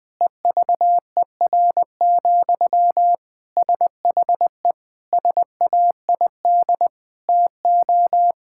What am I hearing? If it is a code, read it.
Morse code, heard as a keyed tone, about 20 wpm, 700 Hz. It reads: EVER, SHE SAID TO
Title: Sentences from Top 1600 Words 20wpm